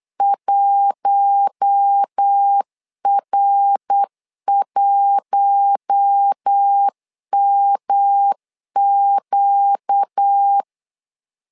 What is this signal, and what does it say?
Morse code, heard as a keyed tone, about 8 wpm, 800 Hz. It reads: 1R1MQ
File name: LEZIONI / 4 Q1 8wpm 3035